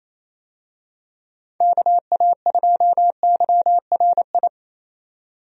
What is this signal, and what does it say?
Morse code, heard as a keyed tone, about 28 wpm, 700 Hz. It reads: KA2YRI
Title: USA Call Signs - 6 Characters - 28wpm